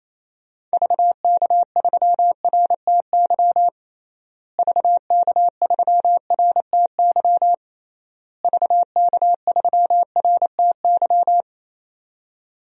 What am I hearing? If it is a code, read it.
Morse code, heard as a keyed tone, about 28 wpm, 700 Hz. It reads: VK3RTY VK3RTY VK3RTY
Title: DXCC-100 Call Signs - 6 Characters - Repeated 3x - 28wpm